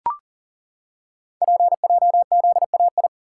Morse code, heard as a keyed tone, about 38 wpm, 690 Hz. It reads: PJZAI
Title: Code Groups - 5 Letters - 40wpm